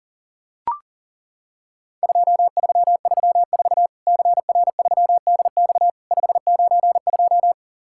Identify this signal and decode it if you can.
Morse code, heard as a keyed tone, about 42 wpm, 700 Hz. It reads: 2334 CR3DX 592
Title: Worked All Europe DX Contest - QTC Exchange - 40wpm